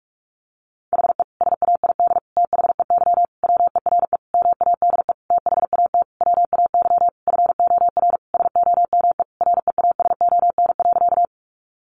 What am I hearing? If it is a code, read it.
Morse code, heard as a keyed tone, about 44 wpm, 700 Hz. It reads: HE SAID THEY WERE MADE THAT WAY FOR SOME REASON.
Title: Sentences from Top 700 Words 45wpm